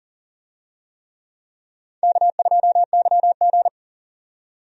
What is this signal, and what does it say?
Morse code, heard as a keyed tone, about 40 wpm, 700 Hz. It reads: K2YG
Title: USA Call Signs - 4 Characters - 40wpm